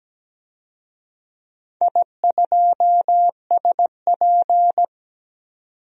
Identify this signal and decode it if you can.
Morse code, heard as a keyed tone, about 17 wpm, 700 Hz. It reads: I2SP
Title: DXCC-100 Call Signs - 4 Characters - 17wpm